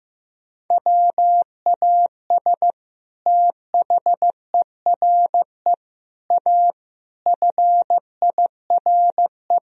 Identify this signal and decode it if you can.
Morse code, heard as a keyed tone, about 16 wpm, 700 Hz. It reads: WAS THERE A FIRE
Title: Sentences from Top 1000 Words - Review Part 4 - 15wpm